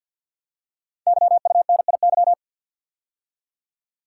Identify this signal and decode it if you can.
Morse code, heard as a keyed tone, about 48 wpm, 700 Hz. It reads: YUNIY
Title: Code Groups - 5 Letters - 50wpm